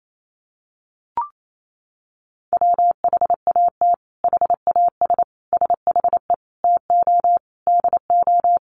Morse code, heard as a keyed tone, about 28 wpm, 700 Hz. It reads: WHAT HAS SHE TO DO
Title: Sentences from Top 100 Words 28wpm